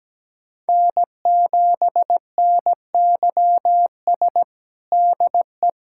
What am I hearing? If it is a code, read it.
Morse code, heard as a keyed tone, about 17 wpm, 700 Hz. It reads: N7NYS DE
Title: USA Callsign DE Callsign 17wpm